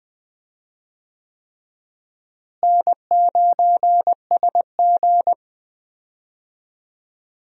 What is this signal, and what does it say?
Morse code, heard as a keyed tone, about 20 wpm, 700 Hz. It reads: N9SG
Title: USA Call Signs - 4 Characters - Repeated 3x - 20wpm